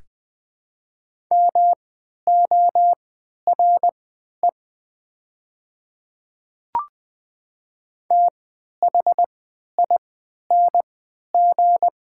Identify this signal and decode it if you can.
Morse code, heard as a keyed tone, about 20 wpm, spaced as for 11 wpm, 700 Hz. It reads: MORE THING
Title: 3x Character Spacing - Top 500 Words - 20wpm